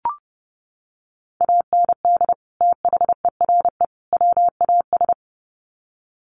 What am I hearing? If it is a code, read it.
Morse code, heard as a keyed tone, about 28 wpm, 700 Hz. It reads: AND THERE WAS
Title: Sets of 3 Words 30wpm